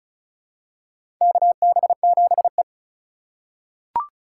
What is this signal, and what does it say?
Morse code, heard as a keyed tone, about 36 wpm, 700 Hz. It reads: KB7E
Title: USA Call Signs - 4 Characters - 2x Character Spacing - 35wpm